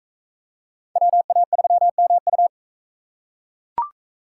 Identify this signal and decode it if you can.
Morse code, heard as a keyed tone, about 43 wpm, 700 Hz. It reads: WA3MU